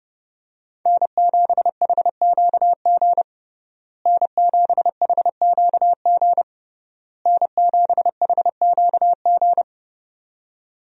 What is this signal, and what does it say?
Morse code, heard as a keyed tone, about 30 wpm, 700 Hz. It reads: N7HQG N7HQG N7HQG
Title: USA Call Signs - 5 Characters - Repeated 3x - 30wpm